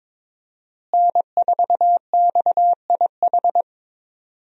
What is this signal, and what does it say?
Morse code, heard as a keyed tone, about 21 wpm, 700 Hz. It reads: N4XIH
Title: USA Call Signs - 5 Characters - Repeated 2x - 22wpm